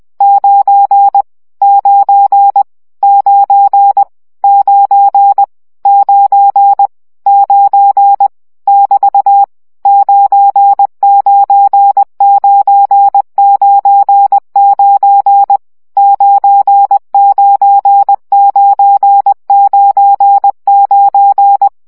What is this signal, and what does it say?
Morse code, heard as a keyed tone, about 20 wpm, 800 Hz. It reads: 9 9 9 9 9 9 <BT> 99999 99999
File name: CW gyakorlas 9 17 baud